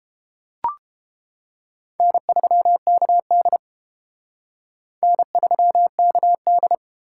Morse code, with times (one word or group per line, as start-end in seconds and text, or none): 2.0-3.6 N3KD
5.0-6.7 N3KD